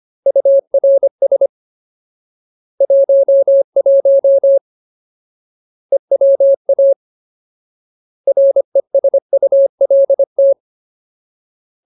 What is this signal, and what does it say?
Morse code, heard as a keyed tone, about 25 wpm, 550 Hz. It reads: URS 11 EWA RESULT
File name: Ultimate Word List / 25WPM Ultimate